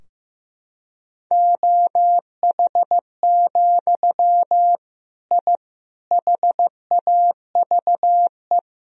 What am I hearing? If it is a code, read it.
Morse code, heard as a keyed tone, about 15 wpm, 700 Hz. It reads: OH, I HAVE
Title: Sentences from Top 1000 Words - Review Part 3 - 15wpm